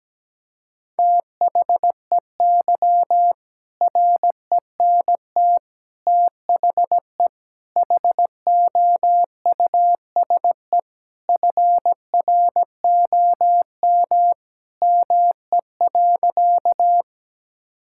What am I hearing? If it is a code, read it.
Morse code, heard as a keyed tone, about 17 wpm, 700 Hz. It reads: THEY RENT THE HOUSE FROM ME.
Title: Sentences from Top 1000 Words - Review Part 5 - 17wpm